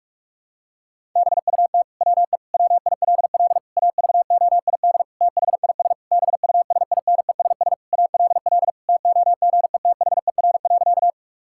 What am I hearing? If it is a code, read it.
Morse code, heard as a keyed tone, about 41 wpm, 700 Hz. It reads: BUT WE WILL AVOID THIS BUSINESS ALL TOGETHER.
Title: Sentences from Top 1000 Words - Review Part 6 - 45wpm